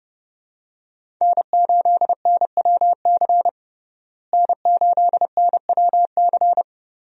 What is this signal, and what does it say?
Morse code, heard as a keyed tone, about 30 wpm, 700 Hz. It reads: N8NWC N8NWC
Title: USA Call Signs - 5 Characters - Repeated 3x - 30wpm